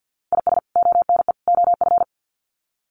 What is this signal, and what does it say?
Morse code, heard as a keyed tone, about 50 wpm, 700 Hz. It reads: IS ONE OF
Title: Sets of 3 Words 50wpm